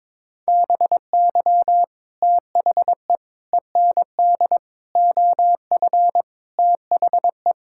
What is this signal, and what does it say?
Morse code, heard as a keyed tone, about 22 wpm, 700 Hz. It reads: BY THE END OF THE
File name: practice / 22wpm / Sentences with 4-Word Phrases - 22wpm